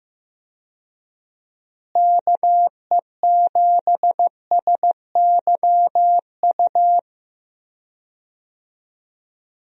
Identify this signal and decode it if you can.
Morse code, heard as a keyed tone, about 15 wpm, 700 Hz. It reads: KE7SYU
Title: USA Call Signs - 6 Characters - Repeated 2x - 15wpm